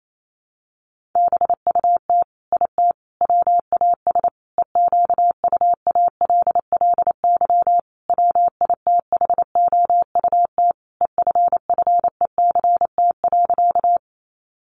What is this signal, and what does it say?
Morse code, heard as a keyed tone, about 28 wpm, 700 Hz. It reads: BUT IT WAS EQUALLY WITHOUT EFFECT.
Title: Sentences from Top 1400 Words 28wpm